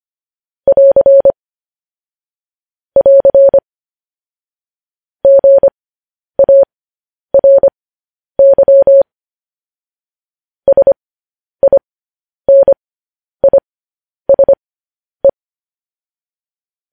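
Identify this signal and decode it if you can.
Morse code, heard as a keyed tone, about 25 wpm, spaced as for 10 wpm, 550 Hz. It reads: <AR> <AR> GARY SINISE
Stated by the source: Morse code news headlines at 10.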